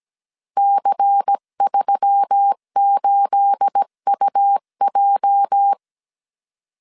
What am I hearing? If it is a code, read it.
Morse code, heard as a keyed tone, about 17 wpm, 800 Hz. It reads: C38UJ
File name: 1500C S B 16wpm